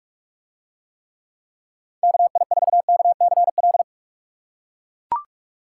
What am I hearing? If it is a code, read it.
Morse code, heard as a keyed tone, about 44 wpm, 700 Hz. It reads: KI4KCL